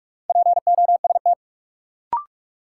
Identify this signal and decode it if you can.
Morse code, heard as a keyed tone, about 47 wpm, 700 Hz. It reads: POST
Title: Sentences from Top 1000 Words - New Word Drill - 45wpm